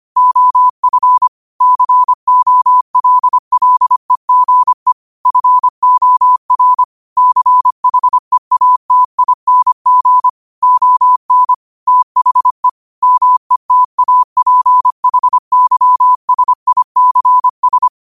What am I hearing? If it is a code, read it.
Morse code, heard as a keyed tone, about 25 wpm, 1000 Hz. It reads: OF COLLEGE FOR CHEATING ON THE METAPHYSICS
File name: qotdfiles / 25wpm